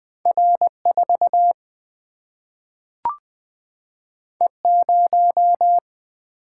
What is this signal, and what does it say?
Morse code, heard as a keyed tone, about 20 wpm, 700 Hz. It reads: R4 E0
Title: Single-Letter Single-Number 20wpm